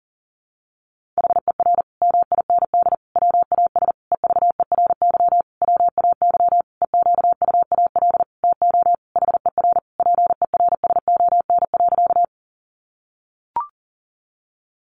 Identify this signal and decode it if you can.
Morse code, heard as a keyed tone, about 40 wpm, 700 Hz. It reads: HER MIND WAS EVERY WAY EQUAL TO HER PERSON.
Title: Sentences from Top 1400 Words 40wpm